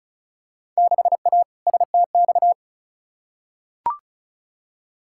Morse code, heard as a keyed tone, about 36 wpm, 700 Hz. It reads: 6A STX
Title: Field Day Exchanges - 35wpm